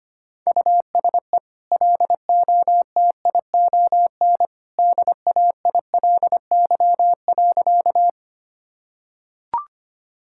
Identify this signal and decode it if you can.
Morse code, heard as a keyed tone, about 25 wpm, 700 Hz. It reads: USE LOTION DAILY.
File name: Easy Sentences with Hints - 25wpm